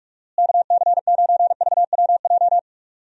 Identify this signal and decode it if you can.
Morse code, heard as a keyed tone, about 42 wpm, 700 Hz. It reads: KC9VWJ
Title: USA Call Signs - 6 Characters - Repeated 3x - 45wpm